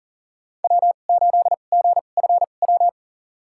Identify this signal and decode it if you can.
Morse code, heard as a keyed tone, about 40 wpm, spaced as for 29 wpm, 700 Hz. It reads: W8GFW
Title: USA Call Signs - 5 Characters - 2x Character Spacing - 40wpm